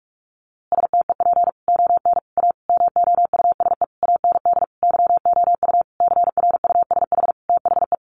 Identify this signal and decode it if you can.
Morse code, heard as a keyed tone, about 44 wpm, 700 Hz. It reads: STEP ON A MOUSE AND YOU CRUSH THE
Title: A Sound Of Thunder 45wpm